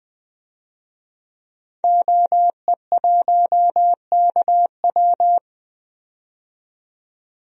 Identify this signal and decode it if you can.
Morse code, heard as a keyed tone, about 20 wpm, 700 Hz. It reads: OE1KW
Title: DXCC-100 Call Signs - 5 Characters - 20wpm